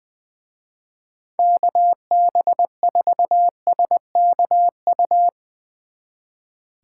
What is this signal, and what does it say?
Morse code, heard as a keyed tone, about 21 wpm, 700 Hz. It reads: KB4SKU
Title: Speed Racing - USA 2x3 Call Signs 20wpm